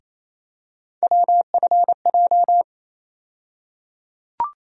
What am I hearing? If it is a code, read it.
Morse code, heard as a keyed tone, about 28 wpm, 700 Hz. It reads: WFJ
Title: USA Call Sign Postfixes - 3 Letters 28wpm